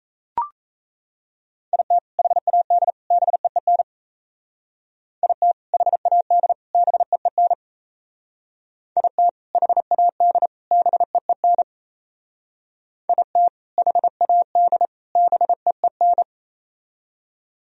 Morse code, heard as a keyed tone, about 36 wpm, 700 Hz. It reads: IT HAD BEEN IT HAD BEEN IT HAD BEEN IT HAD BEEN